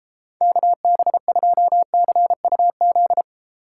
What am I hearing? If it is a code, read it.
Morse code, heard as a keyed tone, about 33 wpm, 700 Hz. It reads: KB2CUZ